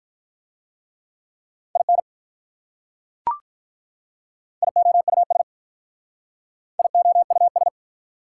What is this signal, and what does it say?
Morse code, heard as a keyed tone, about 43 wpm, 690 Hz. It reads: IN IOUS IOUS
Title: Speed-Racing Prefixes and Suffixes 35wpm